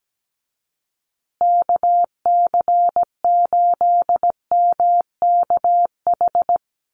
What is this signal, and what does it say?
Morse code, heard as a keyed tone, about 17 wpm, 700 Hz. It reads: KC8MKH
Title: USA Call Signs Repeated 2x 17wpm